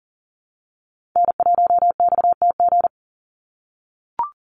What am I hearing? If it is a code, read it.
Morse code, heard as a keyed tone, about 40 wpm, 700 Hz. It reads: N1XTG
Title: USA Call Signs 40wpm